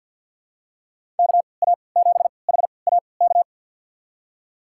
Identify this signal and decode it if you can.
Morse code, heard as a keyed tone, about 52 wpm, spaced as for 29 wpm, 700 Hz. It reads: KA7HAK
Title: USA Call Signs - 6 Characters - 3x Character Spacing - 50wpm